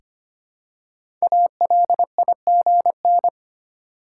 Morse code, heard as a keyed tone, about 25 wpm, 700 Hz. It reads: ALIGN
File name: 5 Letter Words - First 3 Letters Given - 25wpm